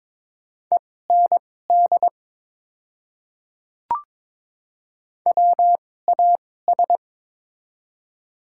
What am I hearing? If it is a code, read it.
Morse code, heard as a keyed tone, about 22 wpm, spaced as for 16 wpm, 700 Hz. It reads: END WAS
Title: 2x Character Spacing - Top 300 Words - 22wpm